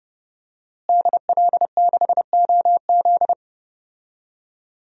Morse code, heard as a keyed tone, about 30 wpm, 700 Hz. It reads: DL6OZ
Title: DXCC-100 Call Signs - 5 Characters - Repeated 2x - 30wpm